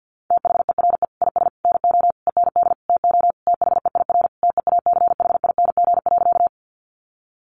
Morse code, heard as a keyed tone, about 50 wpm, 700 Hz. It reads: THERE IS NO END TO THEIR TEACHING.
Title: Sentences from Top 1100 Words 50wpm